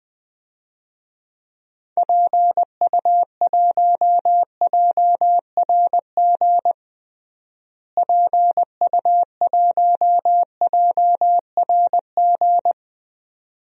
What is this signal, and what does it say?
Morse code, heard as a keyed tone, about 21 wpm, 700 Hz. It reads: PU1JRG PU1JRG
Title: DXCC-100 Call Signs - 6 Characters - Repeated 2x - 20wpm